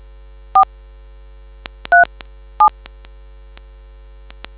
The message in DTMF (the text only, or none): 4 37